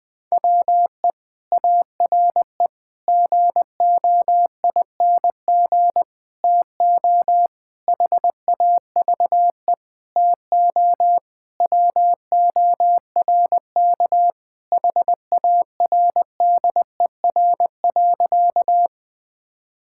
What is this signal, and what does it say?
Morse code, heard as a keyed tone, about 19 wpm, 700 Hz. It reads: WE ARE GOING TO HAVE TO WORK HARDER.
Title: Sentences with 6-Word Phrases - Spoken Beginning - 20wpm